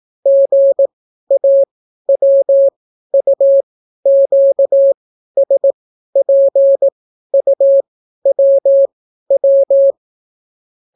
Morse code, heard as a keyed tone, about 18 wpm, spaced as for 12 wpm, 550 Hz. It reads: GAWUQSPUWW